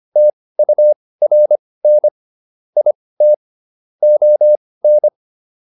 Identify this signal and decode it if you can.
Morse code, heard as a keyed tone, about 25 wpm, spaced as for 18 wpm, 600 Hz. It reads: TURN IT ON